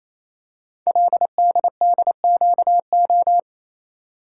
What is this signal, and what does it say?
Morse code, heard as a keyed tone, about 27 wpm, 700 Hz. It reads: LDDQO
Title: Code Groups - 5 Letters - 28wpm